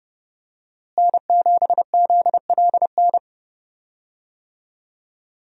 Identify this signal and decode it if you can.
Morse code, heard as a keyed tone, about 30 wpm, 700 Hz. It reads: N7ZLN
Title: USA Call Signs - 5 Characters - Repeated 3x - 30wpm